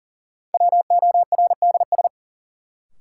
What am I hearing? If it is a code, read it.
Morse code, heard as a keyed tone, about 40 wpm, 700 Hz. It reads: WORDS
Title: Sentences from Top 1000 Words - Review Part 6 - New Word Drill - 40wpm